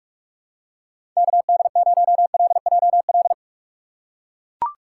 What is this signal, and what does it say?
Morse code, heard as a keyed tone, about 43 wpm, 700 Hz. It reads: KD0LJL
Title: USA Call Signs - 6 Characters - 6x Character Spacing - 45wpm